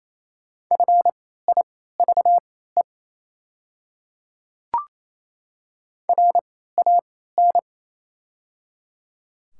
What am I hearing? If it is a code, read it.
Morse code, heard as a keyed tone, about 28 wpm, spaced as for 16 wpm, 700 Hz. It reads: FIVE RAN
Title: 3x Character Spacing - Top 500 Words - 28wpm